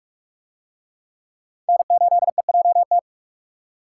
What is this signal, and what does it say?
Morse code, heard as a keyed tone, about 44 wpm, 700 Hz. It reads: N8EJT